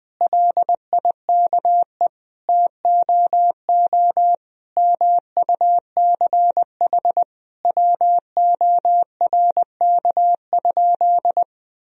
Morse code, heard as a keyed tone, about 20 wpm, 700 Hz. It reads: LIKE TOO MUCH WORK?
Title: Sentences from Top 1000 Words - Review Part 1 - 20wpm